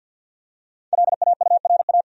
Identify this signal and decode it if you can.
Morse code, heard as a keyed tone, about 52 wpm, 700 Hz. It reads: LAURA
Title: USA Common Names - 50wpm